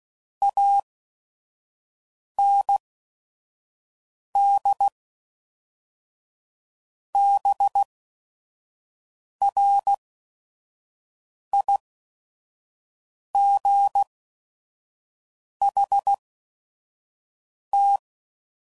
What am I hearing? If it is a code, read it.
Morse code, heard as a keyed tone, about 16 wpm, spaced as for 5 wpm, 800 Hz. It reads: AND BRIGHT
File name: CW Code Groups @5WPM / Weather Sets / Weather 5 5